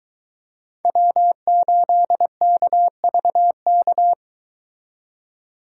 Morse code, heard as a keyed tone, about 22 wpm, 700 Hz. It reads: W8KVK